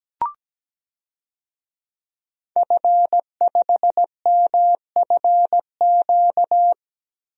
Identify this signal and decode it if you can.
Morse code, heard as a keyed tone, about 17 wpm, 700 Hz. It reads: F5MFQ
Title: DXCC-100 Call Signs - 5 Characters - Repeated 2x - 17wpm